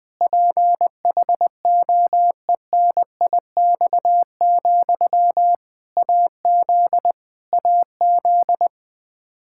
Morse code, heard as a keyed tone, about 20 wpm, 700 Hz. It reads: PHOENIX, AZ AZ
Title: Practice QSO - Exchange 1 - Beginning - 20wpm